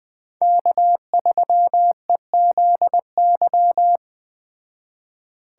Morse code, heard as a keyed tone, about 20 wpm, 700 Hz. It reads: K3EZY
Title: Speed Racing - USA 1x3 Call Signs 17wpm